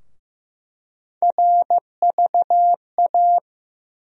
Morse code, heard as a keyed tone, about 15 wpm, 700 Hz. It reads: RVA
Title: USA Call Sign Postfixes - 3 Letters 15wpm